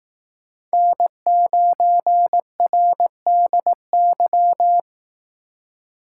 Morse code, heard as a keyed tone, about 18 wpm, 700 Hz. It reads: N9RDY
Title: Speed Racing - USA 1x3 Call Signs 15wpm